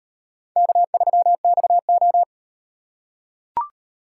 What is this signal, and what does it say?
Morse code, heard as a keyed tone, about 36 wpm, 700 Hz. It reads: K3XO